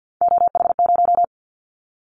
Morse code, heard as a keyed tone, about 50 wpm, 700 Hz. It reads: OH0
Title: Callsign Prefixes for Top 100 DXCC - 50wpm